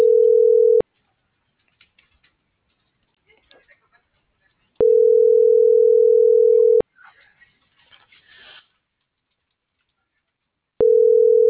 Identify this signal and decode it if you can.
Ringback tone (North America), 440 + 480 Hz.